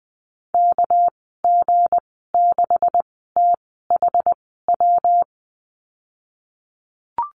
Morse code, heard as a keyed tone, about 20 wpm, spaced as for 15 wpm, 700 Hz. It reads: KG6THW
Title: 238 Lesson 38g - Letter q - Call Signs 20wpm